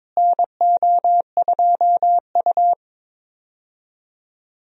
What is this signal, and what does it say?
Morse code, heard as a keyed tone, about 22 wpm, 700 Hz. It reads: NO2U
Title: USA Call Signs - 4 Characters - Repeated 3x - 22wpm